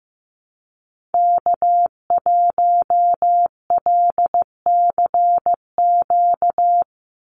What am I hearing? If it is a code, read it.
Morse code, heard as a keyed tone, about 15 wpm, 700 Hz. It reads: K1LCQ
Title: USA Call Signs 15wpm